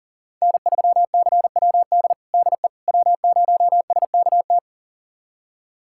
Morse code, heard as a keyed tone, about 40 wpm, 700 Hz. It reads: N3CWD DE W0SKT
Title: USA Callsign DE Callsign - 5 Characters - 40wpm